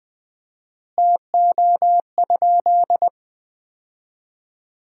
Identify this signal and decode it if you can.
Morse code, heard as a keyed tone, about 20 wpm, 700 Hz. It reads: TO?
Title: 4x Word-Spacing - Sentences from Top 300 Words - 20wpm